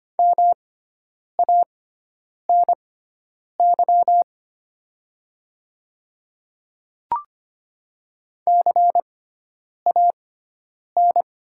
Morse code, heard as a keyed tone, about 25 wpm, spaced as for 9 wpm, 700 Hz. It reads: MANY CAN
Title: 6x Character Spacing - Top 500 Words - 25wpm